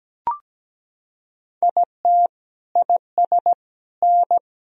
Morse code, heard as a keyed tone, about 17 wpm, 700 Hz. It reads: IT IS N
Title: Sentences from Top 500 Words and Abbreviations 17wpm